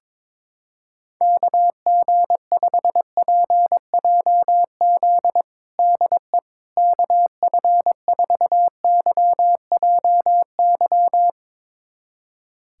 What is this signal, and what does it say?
Morse code, heard as a keyed tone, about 22 wpm, 700 Hz. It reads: KG5PJZ DE KF4YJY
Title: USA Callsign DE Callsign - 6 Characters - 22wpm